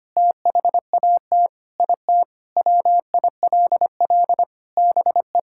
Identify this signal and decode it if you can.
Morse code, heard as a keyed tone, about 25 wpm, 700 Hz. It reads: THAT IT WILL BE
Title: Sentences from Top 1000 Words - Review Part 6 - 25wpm